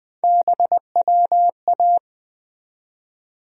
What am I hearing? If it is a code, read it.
Morse code, heard as a keyed tone, about 21 wpm, 700 Hz. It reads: BWA
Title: USA Call Sign Postfixes - 3 Letters 20wpm